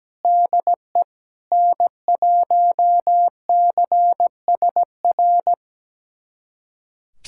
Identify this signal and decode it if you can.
Morse code, heard as a keyed tone, about 17 wpm, 700 Hz. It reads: DE N1CSR
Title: USA Callsign DE Callsign 17wpm